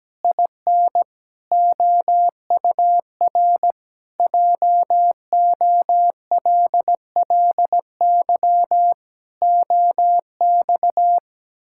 Morse code, heard as a keyed tone, about 17 wpm, 700 Hz. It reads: IN OUR JOLLY OX
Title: Pangram Challenge v2 - 17wpm